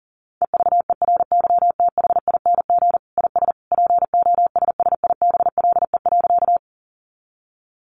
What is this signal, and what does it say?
Morse code, heard as a keyed tone, about 40 wpm, 700 Hz. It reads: EVERYTHING IS POSSIBLE.
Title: Sentences from Top 700 Words 40wpm